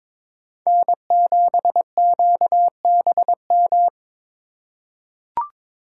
Morse code, heard as a keyed tone, about 22 wpm, 690 Hz. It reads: N7QBM
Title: USA Call Signs - 5 Characters - 2x Character Spacing - 22wpm